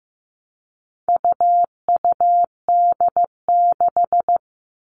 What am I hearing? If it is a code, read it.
Morse code, heard as a keyed tone, about 15 wpm, 700 Hz. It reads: UUD6
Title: All Cut Numbers - 4-Digit 15wpm